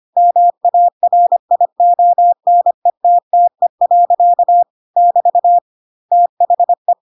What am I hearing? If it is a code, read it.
Morse code, heard as a keyed tone, about 25 wpm, 700 Hz. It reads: MARIONETTE. <BT> THE
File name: Pinocchio / Pinocchio ch7 25 25 ik1hge